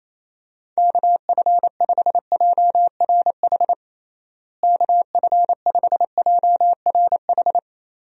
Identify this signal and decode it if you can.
Morse code, heard as a keyed tone, about 28 wpm, 700 Hz. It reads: KF5JRH KF5JRH
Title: USA Call Signs - 6 Characters - Repeated 2x - 28wpm